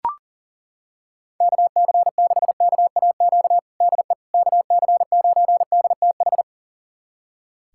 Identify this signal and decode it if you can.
Morse code, heard as a keyed tone, about 40 wpm, 700 Hz. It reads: KC6KAQ DE KC9DTH
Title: USA Callsign DE Callsign - 6 Characters - 40wpm